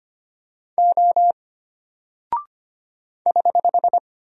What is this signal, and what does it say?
Morse code, heard as a keyed tone, about 25 wpm, 700 Hz. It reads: O <HH>
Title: Single Character (Letter:Number:Punctuation:Prosign) Alt 25wpm